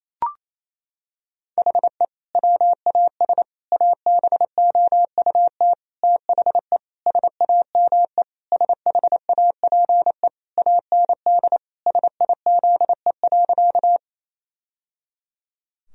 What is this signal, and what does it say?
Morse code, heard as a keyed tone, about 27 wpm, 700 Hz. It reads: HE WAS ABOUT THE SAME SHAPE AND SIZE.
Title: Sentences from Top 1000 Words - Review Part 3 - 28wpm